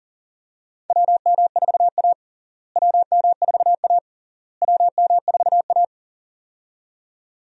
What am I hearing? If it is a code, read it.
Morse code, heard as a keyed tone, about 40 wpm, 700 Hz. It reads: WM4A WM4A WM4A
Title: USA Call Signs - 4 Characters - Repeated 3x - 40wpm